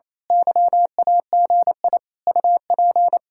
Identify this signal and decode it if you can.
Morse code, heard as a keyed tone, about 27 wpm, 700 Hz. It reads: YAGI UP
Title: Speed-Racing Common QSO Phrases - 28wpm